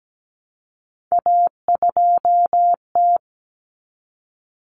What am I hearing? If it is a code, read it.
Morse code, heard as a keyed tone, about 17 wpm, 700 Hz. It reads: A2T
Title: Common Cut Numbers - 3-Digit 17wpm